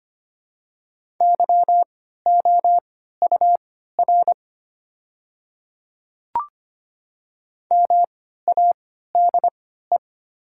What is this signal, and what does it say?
Morse code, heard as a keyed tone, about 25 wpm, spaced as for 14 wpm, 700 Hz. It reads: YOUR MADE
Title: 3x Character Spacing - Top 200 Words - 25wpm